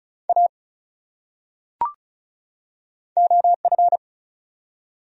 Morse code, heard as a keyed tone, about 36 wpm, 700 Hz. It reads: A OF
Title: Top 100 Words 35wpm